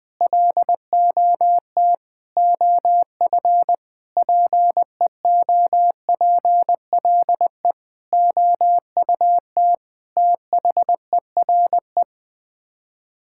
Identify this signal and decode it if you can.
Morse code, heard as a keyed tone, about 20 wpm, 700 Hz. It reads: LOT OF PEOPLE OUT THERE
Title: Sets of 8 Words - 20wpm